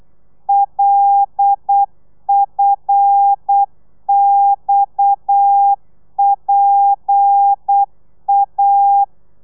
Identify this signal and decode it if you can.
Morse code, heard as a keyed tone, about 8 wpm, 800 Hz. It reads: LFXPA